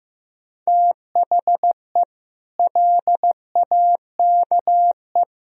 Morse code, heard as a keyed tone, about 16 wpm, 700 Hz. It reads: THE LAKE
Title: Sentences from Top 1000 Words - Review Part 3 - 15wpm